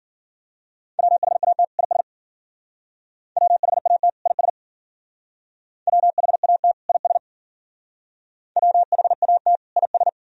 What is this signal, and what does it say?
Morse code, heard as a keyed tone, about 52 wpm, 700 Hz. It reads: WHAT IS WHAT IS WHAT IS WHAT IS